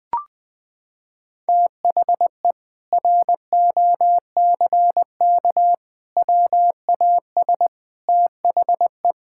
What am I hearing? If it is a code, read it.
Morse code, heard as a keyed tone, about 21 wpm, 700 Hz. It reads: THE ROCK WAS THE
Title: Sentences with 4-Word Phrases - 20wpm